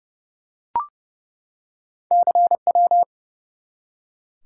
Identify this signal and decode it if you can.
Morse code, heard as a keyed tone, about 30 wpm, 700 Hz. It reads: CW
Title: Top 180 QSO Elements 30wpm